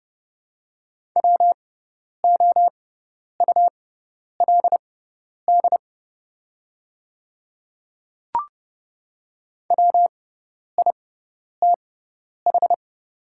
Morse code, heard as a keyed tone, about 30 wpm, spaced as for 10 wpm, 700 Hz. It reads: WOULD WITH
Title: 6x Character Spacing - Top 100 Words - 30wpm